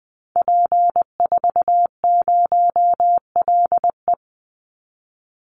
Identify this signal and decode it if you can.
Morse code, heard as a keyed tone, about 20 wpm, 700 Hz. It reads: P40LE
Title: Call Signs Repeated 2x - 20wpm